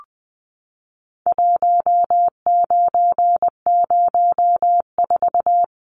Morse code, heard as a keyed tone, about 20 wpm, 700 Hz. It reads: 1904
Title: Numbers - 4-Digit 20wpm